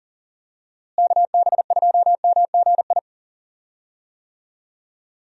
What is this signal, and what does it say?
Morse code, heard as a keyed tone, about 37 wpm, 700 Hz. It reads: KB2MGI
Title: USA Call Signs - 6 Characters - Repeated 2x - 40wpm